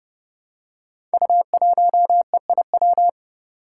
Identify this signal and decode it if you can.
Morse code, heard as a keyed tone, about 31 wpm, 700 Hz. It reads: U1EIW
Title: Five Challenging Letter-Number Combinations 30wpm